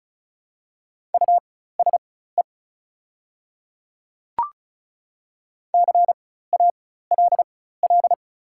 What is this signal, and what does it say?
Morse code, heard as a keyed tone, about 36 wpm, spaced as for 16 wpm, 700 Hz. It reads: USE CALL
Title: 4x Character Spacing - Top 100 Words - 35wpm